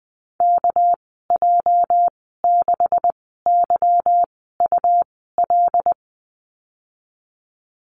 Morse code, heard as a keyed tone, about 20 wpm, spaced as for 15 wpm, 700 Hz. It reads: KJ6YUL
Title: 219 Lesson 35g - Number 0 - Call Signs 20wpm